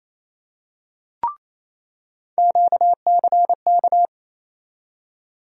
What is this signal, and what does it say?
Morse code, heard as a keyed tone, about 28 wpm, 700 Hz. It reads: QCK
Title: USA Call Sign Postfixes - 3 Letters 28wpm